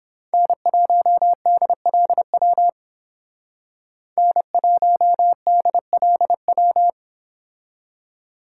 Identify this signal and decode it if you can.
Morse code, heard as a keyed tone, about 30 wpm, 700 Hz. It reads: N1DLW N1DLW